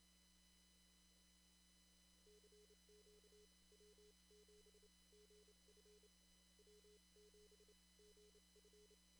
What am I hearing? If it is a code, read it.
Morse code, heard as a keyed tone, about 28 wpm, spaced as for 23 wpm, 420 Hz. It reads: CQ W7GF W7GF
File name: CW / kt5j wpx cw-op-k5pi